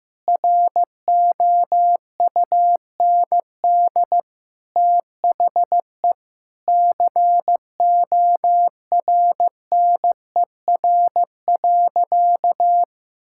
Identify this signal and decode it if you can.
Morse code, heard as a keyed tone, about 16 wpm, 700 Hz. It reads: ROUND THE CORNER.
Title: Sentences from Top 1600 Words 15wpm